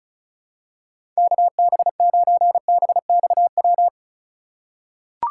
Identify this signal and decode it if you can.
Morse code, heard as a keyed tone, about 34 wpm, 700 Hz. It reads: KB9BXW
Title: USA Call Signs - 2x Character Spacing - 35wpm